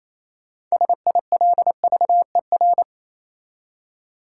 Morse code, heard as a keyed tone, about 28 wpm, 700 Hz. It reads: SILVER
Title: Six Letter Words - 28wpm